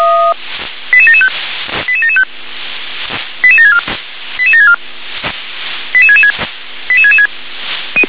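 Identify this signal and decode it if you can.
Five-tone selective call (ZVEI), heard as quick runs of five tones, tones of about 70 ms. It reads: 88775 88775 88764 88764 88666 88666
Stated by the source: ZVEI five-tone selective call received by SDR.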